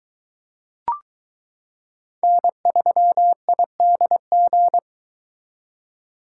Morse code, heard as a keyed tone, about 22 wpm, 700 Hz. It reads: N3IDG